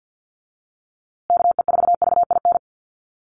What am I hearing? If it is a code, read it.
Morse code, heard as a keyed tone, about 50 wpm, 700 Hz. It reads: KE4VIN
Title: USA Call Signs Repeated 3x 50wpm